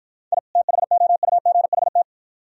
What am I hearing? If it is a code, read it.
Morse code, heard as a keyed tone, about 48 wpm, 700 Hz. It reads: I THOUGHT